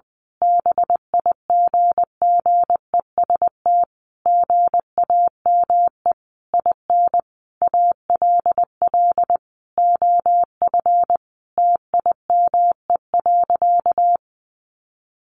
Morse code, heard as a keyed tone, about 20 wpm, 700 Hz. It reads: BIGGEST GAME IN ALL OF TIME.
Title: A Sound Of Thunder 20wpm